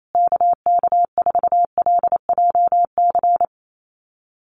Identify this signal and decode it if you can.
Morse code, heard as a keyed tone, about 28 wpm, 700 Hz. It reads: KK4LJC
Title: USA Call Signs Repeated 3x 28wpm